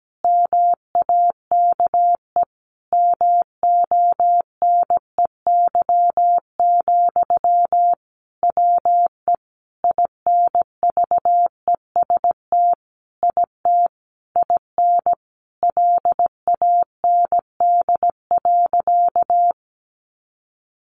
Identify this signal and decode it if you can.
Morse code, heard as a keyed tone, about 17 wpm, 700 Hz. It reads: MAKE MONEY, WE INVEST IT IN LAND.
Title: Sentences from Top 1100 Words 17wpm